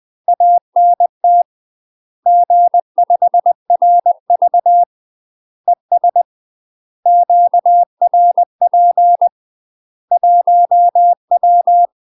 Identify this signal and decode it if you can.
Morse code, heard as a keyed tone, about 20 wpm, 700 Hz. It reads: ANT G5RV ES QRP 1W
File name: UK MASTER QSOs TRAINING / 20WPM / UK MASTER QSOs 0066